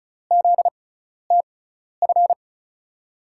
Morse code, heard as a keyed tone, about 36 wpm, spaced as for 12 wpm, 700 Hz. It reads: ZTF
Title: USA Call Signs - 6x Character Spacing - 35wpm